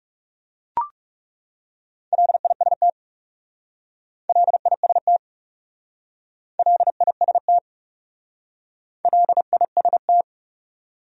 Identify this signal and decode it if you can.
Morse code, heard as a keyed tone, about 38 wpm, 700 Hz. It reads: LIST LIST LIST LIST